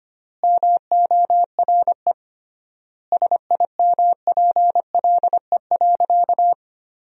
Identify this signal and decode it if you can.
Morse code, heard as a keyed tone, about 25 wpm, 700 Hz. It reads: MORE SIMPLE.
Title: 3x Word-Spacing - Sentences from Top 500 Words - 25wpm